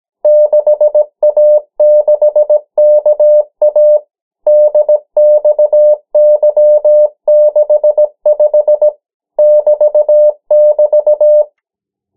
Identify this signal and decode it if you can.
Morse code, heard as a keyed tone, about 17 wpm, 600 Hz. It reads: 6A6KA DXY65 <BT><BT>
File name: morse mp3 / 14 urok 13